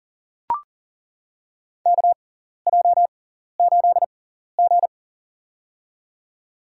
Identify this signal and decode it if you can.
Morse code, heard as a keyed tone, about 40 wpm, spaced as for 14 wpm, 700 Hz. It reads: KJ8G
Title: USA Call Signs - 4 Characters - 6x Character Spacing - 40wpm